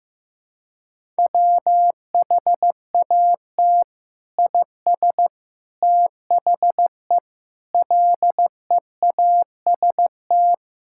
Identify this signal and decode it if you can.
Morse code, heard as a keyed tone, about 15 wpm, 700 Hz. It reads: WHAT IS THE LEAST
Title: Sentences from Top 1000 Words - Review Part 6 - 15wpm